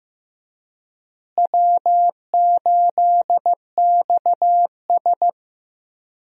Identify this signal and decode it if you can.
Morse code, heard as a keyed tone, about 15 wpm, 700 Hz. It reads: W8XS
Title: USA Call Signs - 4 Characters - Repeated 3x - 15wpm